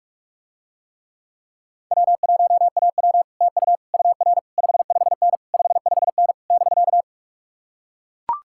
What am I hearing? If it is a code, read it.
Morse code, heard as a keyed tone, about 44 wpm, 700 Hz. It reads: W1AW TU UR 55N 55N <BK>